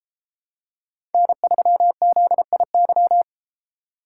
Morse code, heard as a keyed tone, about 33 wpm, 700 Hz. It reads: N3ZIY